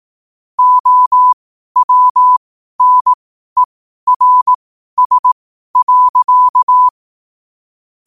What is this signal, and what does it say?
Morse code, heard as a keyed tone, about 18 wpm, spaced as for 12 wpm, 1000 Hz. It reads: OWNERS.
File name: qotdfiles / 12wpm